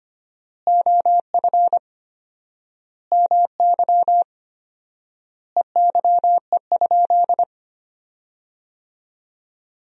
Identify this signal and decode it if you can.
Morse code, heard as a keyed tone, about 25 wpm, 700 Hz. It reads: OF MY EYE?
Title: 4x Word-Spacing - Sentences from Top 500 Words - 25wpm